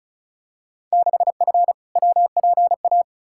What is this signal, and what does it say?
Morse code, heard as a keyed tone, about 36 wpm, 700 Hz. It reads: 6F WPA
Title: Field Day Exchanges Encore - 35wpm